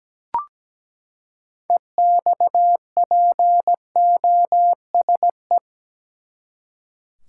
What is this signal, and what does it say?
Morse code, heard as a keyed tone, about 17 wpm, 700 Hz. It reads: EXPOSE
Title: Six Letter Words - 17wpm